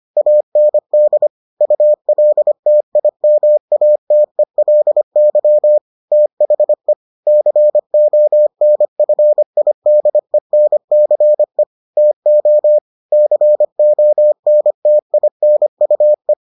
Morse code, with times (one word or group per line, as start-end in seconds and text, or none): 0.2-1.3 AND
1.6-5.8 ULTIMATELY
6.1-6.9 THE
7.3-11.6 CONFIDENCE
12.0-12.8 TO
13.1-16.3 CONTINUE